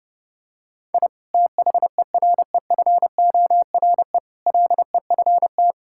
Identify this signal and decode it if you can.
Morse code, heard as a keyed tone, about 32 wpm, 700 Hz. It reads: I THEREFORE LEFT
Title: Sentences from Top 1000 Words - Review Part 5 - 30wpm